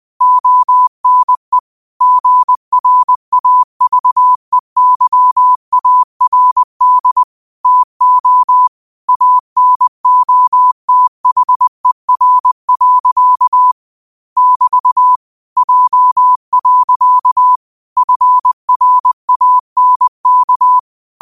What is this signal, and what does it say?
Morse code, heard as a keyed tone, about 20 wpm, 1000 Hz. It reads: ONE GRAVEYARD TO ANOTHER. <BT> J. FRANK